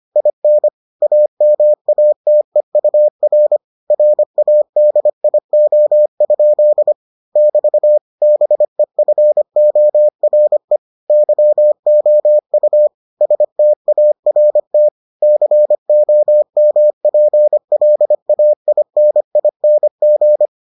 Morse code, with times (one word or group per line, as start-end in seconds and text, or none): 0.2-0.7 IN
1.0-3.6 AMATEUR
3.9-6.9 RADIO?
7.4-8.0 <BT>
8.2-10.8 BEFORE
11.1-12.9 YOU
13.2-14.9 START
15.2-20.5 COMPLAINING